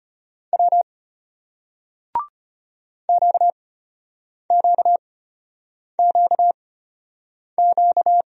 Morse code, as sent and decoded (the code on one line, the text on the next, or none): .-- / --.- --.- --.- --.-
W QQQQ